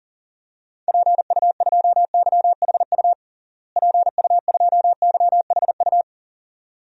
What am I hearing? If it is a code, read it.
Morse code, heard as a keyed tone, about 37 wpm, 700 Hz. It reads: PU2YHU PU2YHU
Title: DXCC-100 Call Signs - 6 Characters - Repeated 2x - 40wpm